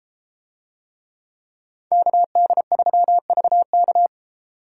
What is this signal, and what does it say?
Morse code, heard as a keyed tone, about 32 wpm, 700 Hz. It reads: KD3VK